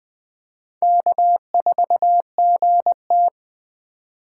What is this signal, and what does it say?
Morse code, heard as a keyed tone, about 20 wpm, 700 Hz. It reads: K4GT
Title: Speed Racing - USA 1x2 Call Signs 17wpm